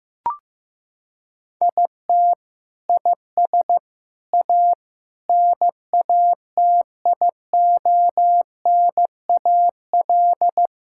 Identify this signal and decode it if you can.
Morse code, heard as a keyed tone, about 16 wpm, 700 Hz. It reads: IT IS A NATIONAL
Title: Sentences from Top 1000 Words - Review Part 5 - 15wpm